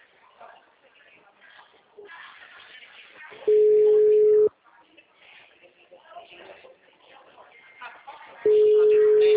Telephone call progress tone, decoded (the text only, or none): ringback tone